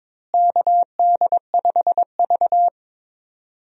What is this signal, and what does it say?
Morse code, heard as a keyed tone, about 22 wpm, 700 Hz. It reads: KD5V
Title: USA Call Signs - 4 Characters - Repeated 3x - 22wpm